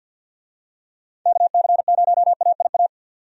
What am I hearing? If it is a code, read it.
Morse code, heard as a keyed tone, about 53 wpm, 700 Hz. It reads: KC0AIA
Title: USA Call Signs - 6 Characters - 50wpm